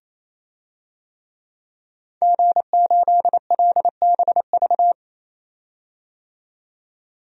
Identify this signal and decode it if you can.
Morse code, heard as a keyed tone, about 28 wpm, 700 Hz. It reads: G8LBV
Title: DXCC-100 Call Signs - 5 Characters - 28wpm